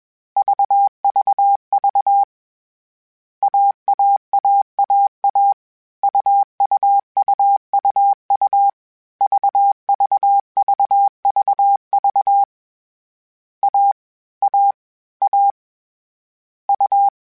Morse code, heard as a keyed tone, about 21 wpm, 800 Hz. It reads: VVV AAAAA UUUUU VVVVV A A A U
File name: CW Code Groups-mp3s / Sets mp3s / AUV AUV4 sp @20